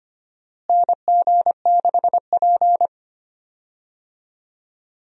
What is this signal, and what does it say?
Morse code, heard as a keyed tone, about 25 wpm, 700 Hz. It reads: NG6P
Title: USA Call Signs - 4 Characters - Repeated 3x - 25wpm